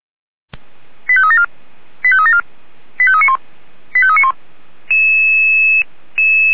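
Five-tone selective call (ZVEI), heard as quick runs of five tones, tones of about 70 ms. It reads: 85374 85374 85391 85391
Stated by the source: Five-tone ZVEI call received on SDR, fire brigade Feuerwehr Orlamünde.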